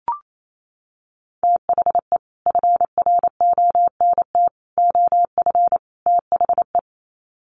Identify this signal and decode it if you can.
Morse code, heard as a keyed tone, about 28 wpm, 700 Hz. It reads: THE FRONT OF THE
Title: Sets of 4 Words 28wpm